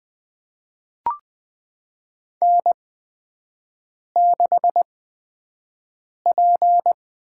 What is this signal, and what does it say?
Morse code, heard as a keyed tone, about 21 wpm, spaced as for 6 wpm, 700 Hz. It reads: N6P
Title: USA Call Signs - 5 Characters - 8x Character Spacing - 20wpm